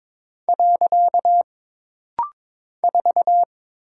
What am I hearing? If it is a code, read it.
Morse code, heard as a keyed tone, about 22 wpm, 700 Hz. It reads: . 4
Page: This rapid-fire timing is geared to push your ICR (Instant Character Recognition) skills.